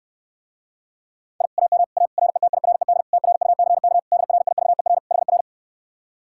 Morse code, heard as a keyed tone, about 69 wpm, 700 Hz. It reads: I AM A RETIRED TRUCK DRIVER HR